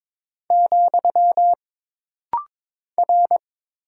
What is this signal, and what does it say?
Morse code, heard as a keyed tone, about 22 wpm, 700 Hz. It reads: , R
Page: This rapid-fire timing is geared to push your ICR (Instant Character Recognition) skills.